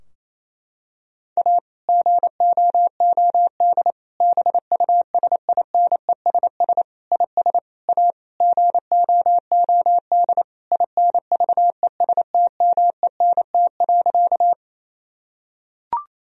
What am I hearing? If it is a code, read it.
Morse code, heard as a keyed tone, about 28 wpm, 700 Hz. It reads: A GOOD BUSINESS IS A GOOD INVESTMENT.
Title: Sentences from Top 1000 Words - Review Part 4 - 28wpm